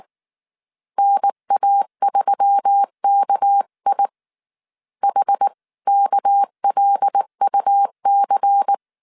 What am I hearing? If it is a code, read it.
Morse code, heard as a keyed tone, about 19 wpm, 800 Hz. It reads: NA3KI HKLUC
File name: CW / DET / 1 6 1500C S C 18wpm 2356